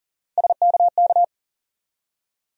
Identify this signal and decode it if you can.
Morse code, heard as a keyed tone, about 40 wpm, 700 Hz. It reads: SKK
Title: USA Call Sign Postfixes - 3 Letters 40wpm